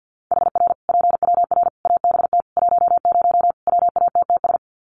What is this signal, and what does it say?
Morse code, heard as a keyed tone, about 48 wpm, 700 Hz. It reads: HR PWR ABT 10 WATTS
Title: Common QSO Phrases 50wpm